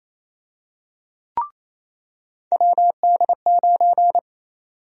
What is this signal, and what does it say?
Morse code, heard as a keyed tone, about 28 wpm, 700 Hz. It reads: WD9
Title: USA Call Sign Prefixes - 28wpm